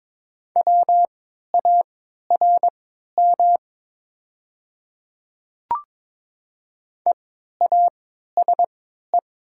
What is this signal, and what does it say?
Morse code, heard as a keyed tone, about 22 wpm, spaced as for 13 wpm, 700 Hz. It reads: WARM EASE
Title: 3x Character Spacing - Top 500 Words - 22wpm